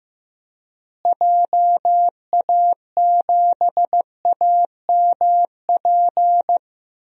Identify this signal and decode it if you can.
Morse code, heard as a keyed tone, about 15 wpm, 700 Hz. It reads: JA7AMP
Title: DXCC-100 Call Signs - 6 Characters - Repeated 3x - 15wpm